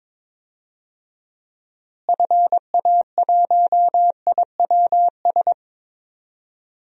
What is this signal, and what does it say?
Morse code, heard as a keyed tone, about 22 wpm, 700 Hz. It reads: FA1IWS
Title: DXCC-100 Call Signs - 6 Characters - Repeated 3x - 22wpm